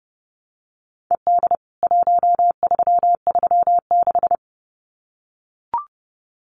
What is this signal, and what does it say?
Morse code, heard as a keyed tone, about 30 wpm, 700 Hz. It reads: ED 1336
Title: CWT Exchange 30wpm